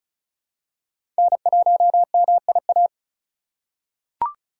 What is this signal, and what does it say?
Morse code, heard as a keyed tone, about 34 wpm, 700 Hz. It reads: N1MIA
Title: USA Call Signs - 5 Characters - 4x Character Spacing - 35wpm